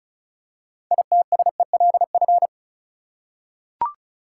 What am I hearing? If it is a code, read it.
Morse code, heard as a keyed tone, about 34 wpm, 700 Hz. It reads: ITSELF
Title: Six Letter Words - 35wpm